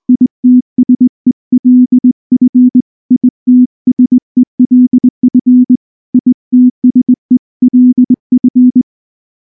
Morse code, heard as a keyed tone, about 20 wpm, 250 Hz. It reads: ITSELF ITSELF ITSELF
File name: morse / morse-training-20WPM-6